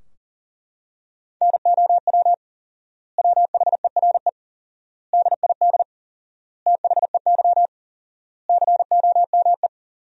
Morse code, heard as a keyed tone, about 37 wpm, 700 Hz. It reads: NOW WHERE DID THEY COME
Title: 4x Word-Spacing - Sentences from Top 200 Words - 40wpm